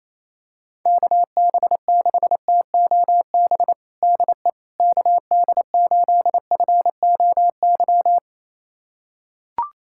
Morse code, heard as a keyed tone, about 28 wpm, 700 Hz. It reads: KB6TOB DE KD8FOY
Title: USA Callsign DE Callsign 28wpm